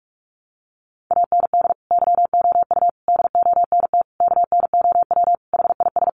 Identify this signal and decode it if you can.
Morse code, heard as a keyed tone, about 44 wpm, 700 Hz. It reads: AND YOU DONT KNOW HIS
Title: Sentences from Top 1000 Words 45wpm